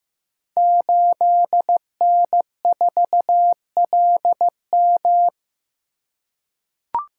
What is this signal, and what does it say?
Morse code, heard as a keyed tone, about 15 wpm, 700 Hz. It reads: 8N4LM
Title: Code Groups - 5 Letter:Numbers - 15wpm